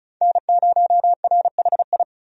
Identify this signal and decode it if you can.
Morse code, heard as a keyed tone, about 36 wpm, 700 Hz. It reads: N0RHI
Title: USA Call Signs - 5 Characters - 35wpm